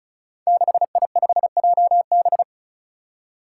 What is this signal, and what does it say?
Morse code, heard as a keyed tone, about 36 wpm, 700 Hz. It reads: 6I5JB
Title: Five Challenging Letter-Number Combinations 35wpm